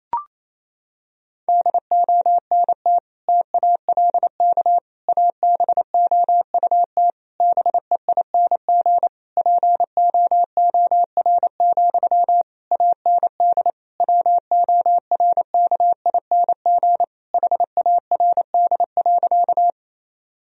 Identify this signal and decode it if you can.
Morse code, heard as a keyed tone, about 28 wpm, 700 Hz. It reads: DONT TALK ABOUT BEING POOR, AND WORKING HARD.
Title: Sentences from Top 1000 Words - Review Part 4 - 28wpm